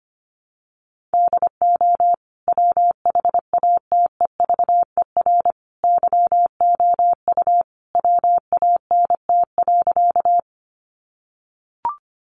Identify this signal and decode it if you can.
Morse code, heard as a keyed tone, about 25 wpm, 700 Hz. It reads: DO WHATEVER YOU WANT.
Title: Sentences from Top 1100 Words 25wpm